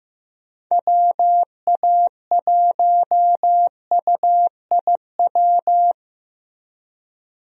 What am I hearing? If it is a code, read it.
Morse code, heard as a keyed tone, about 15 wpm, 700 Hz. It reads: WA1UIW
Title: USA Call Signs - 6 Characters - Repeated 3x - 15wpm